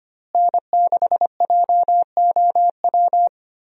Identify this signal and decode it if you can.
Morse code, heard as a keyed tone, about 25 wpm, 700 Hz. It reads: N6JOW
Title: USA Call Signs - 5 Characters - Repeated 2x - 25wpm